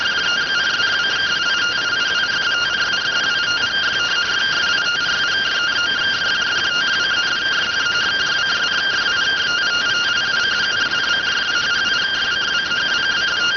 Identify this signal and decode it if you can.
Two-tone teleprinter signal, about 170 Hz shift.